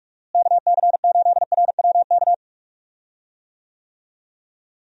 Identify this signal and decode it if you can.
Morse code, heard as a keyed tone, about 46 wpm, 700 Hz. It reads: KC8RWK
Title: USA Call Signs - 6 Characters - Repeated 2x - 45wpm